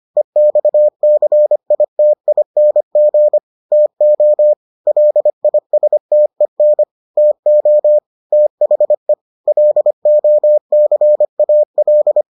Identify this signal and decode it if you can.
Morse code, heard as a keyed tone, about 25 wpm, 600 Hz. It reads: EXCITING TO LISTEN TO THE LOCAL